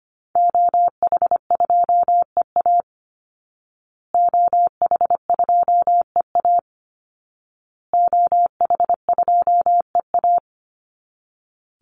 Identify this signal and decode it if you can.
Morse code, heard as a keyed tone, about 25 wpm, 700 Hz. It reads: OH2EA OH2EA OH2EA
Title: Call Signs Repeated 2x - 25wpm